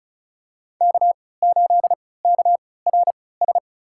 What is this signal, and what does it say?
Morse code, heard as a keyed tone, about 34 wpm, spaced as for 20 wpm, 700 Hz. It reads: K8KRS
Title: USA Call Signs - 5 Characters - 3x Character Spacing - 35wpm